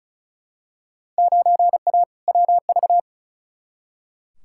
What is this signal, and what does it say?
Morse code, heard as a keyed tone, about 36 wpm, 700 Hz. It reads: 9A WV
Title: Field Day Exchanges - 35wpm